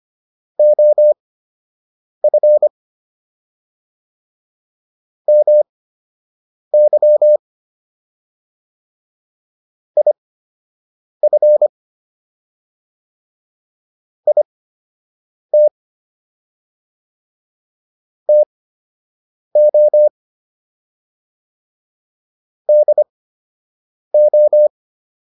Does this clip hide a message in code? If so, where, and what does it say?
Yes: Morse code from 0.6 to 24.7 s, heard as a keyed tone, about 25 wpm, spaced as for 7 wpm, 600 Hz. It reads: OF MY IF IT TO DO